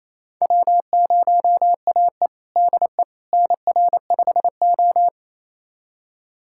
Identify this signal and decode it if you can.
Morse code, heard as a keyed tone, about 28 wpm, 700 Hz. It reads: W0AE DE NR5O
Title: USA Callsign DE Callsign - 4 Characters - 28wpm